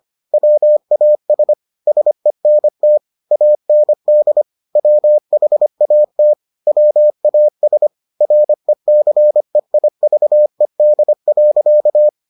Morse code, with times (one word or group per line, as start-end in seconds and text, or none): 0.3-1.5 WAS
1.9-3.0 SENT
3.3-4.4 AND
4.8-6.3 WHAT
6.7-7.9 WAS
8.2-12.1 RECEIVED.